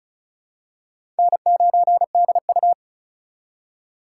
Morse code, heard as a keyed tone, about 36 wpm, 700 Hz. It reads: N9DU
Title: USA Call Signs - 4 Characters - 4x Character Spacing - 35wpm